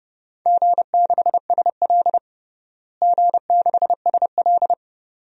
Morse code, heard as a keyed tone, about 30 wpm, 700 Hz. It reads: G6SL G6SL
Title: DXCC-100 Call Signs - 4 Characters - Repeated 2x - 30wpm